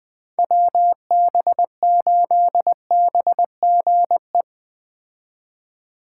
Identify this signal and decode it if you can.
Morse code, heard as a keyed tone, about 21 wpm, 700 Hz. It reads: WB8BGE